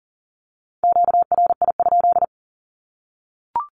Morse code, heard as a keyed tone, about 40 wpm, 700 Hz. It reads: QRI?
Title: Q-Codes 40wpm